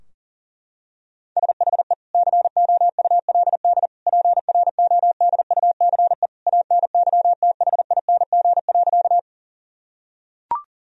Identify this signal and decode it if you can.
Morse code, heard as a keyed tone, about 40 wpm, 700 Hz. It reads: SHE COULD PRODUCE ANYTHING.
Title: Sentences from Top 1000 Words - Review Part 3 - 40wpm